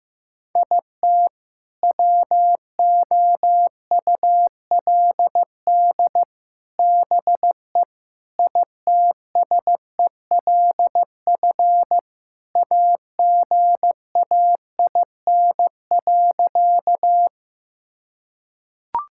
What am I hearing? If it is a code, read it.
Morse code, heard as a keyed tone, about 15 wpm, 700 Hz. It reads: IT WOULD BE ITSELF AGAIN.
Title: Sentences from Top 1000 Words - Review Part 5 - 15wpm